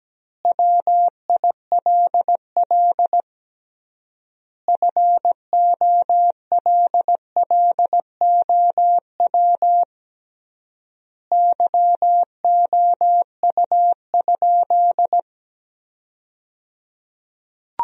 Morse code, heard as a keyed tone, about 17 wpm, 700 Hz. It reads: WILL FOLLOW YOU?
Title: 3x Word-Spacing - Sentences from Top 200 Words - 17wpm